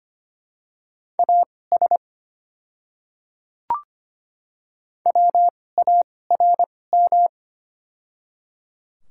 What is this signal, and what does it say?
Morse code, heard as a keyed tone, about 25 wpm, spaced as for 18 wpm, 700 Hz. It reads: AS WARM
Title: 2x Character Spacing - Top 500 Words - 25wpm